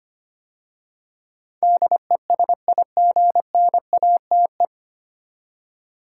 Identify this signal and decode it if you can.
Morse code, heard as a keyed tone, about 25 wpm, 700 Hz. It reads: DESIGNATE
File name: practice / 25wpm / 9 Letter Words - First 3 Letters Given - 25wpm